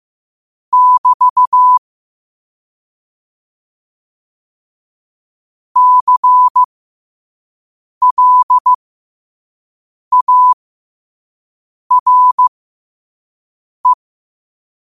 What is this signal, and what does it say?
Morse code, heard as a keyed tone, about 15 wpm, spaced as for 5 wpm, 1000 Hz. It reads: <BT> CLARE
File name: qotdfiles / 5wpm